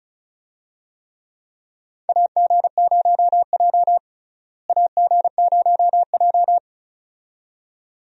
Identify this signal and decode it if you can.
Morse code, heard as a keyed tone, about 36 wpm, 700 Hz. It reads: AG0J AG0J
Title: USA Call Signs - 4 Characters - Repeated 2x - 35wpm